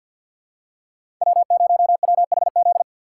Morse code, heard as a keyed tone, about 52 wpm, 700 Hz. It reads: W0WHZ
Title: USA Call Signs - 5 Characters - 2x Character Spacing - 50wpm